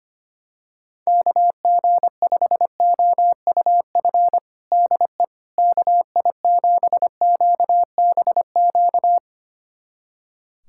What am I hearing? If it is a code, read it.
Morse code, heard as a keyed tone, about 25 wpm, 700 Hz. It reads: KG5OUF DE KI7QBQ
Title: USA Callsign DE Callsign 25wpm